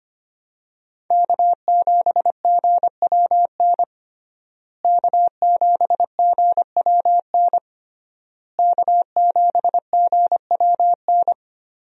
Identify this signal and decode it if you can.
Morse code, heard as a keyed tone, about 25 wpm, 700 Hz. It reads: K7GWN K7GWN K7GWN
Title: USA Call Signs - 5 Characters - Repeated 3x - 25wpm